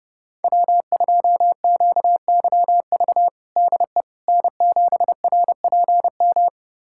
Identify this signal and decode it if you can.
Morse code, heard as a keyed tone, about 32 wpm, 700 Hz. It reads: W2QYV DE N7RPM
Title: USA Callsign DE Callsign - 5 Characters - 30wpm